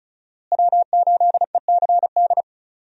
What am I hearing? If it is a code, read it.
Morse code, heard as a keyed tone, about 36 wpm, 700 Hz. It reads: W8ECD
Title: USA Call Signs - 5 Characters - Repeated 3x - 35wpm